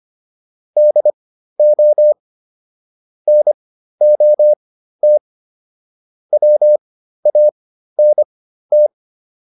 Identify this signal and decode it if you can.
Morse code, heard as a keyed tone, about 25 wpm, spaced as for 13 wpm, 600 Hz. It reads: DO NOT WANT